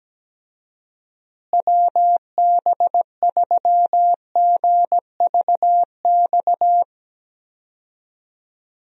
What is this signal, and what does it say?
Morse code, heard as a keyed tone, about 17 wpm, 700 Hz. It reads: WB3GVX
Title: USA Call Signs - 6 Characters - Repeated 2x - 17wpm